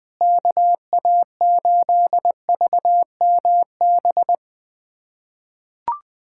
Morse code, heard as a keyed tone, about 20 wpm, 700 Hz. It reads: KA8VMB
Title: USA Callsign DE Callsign 20wpm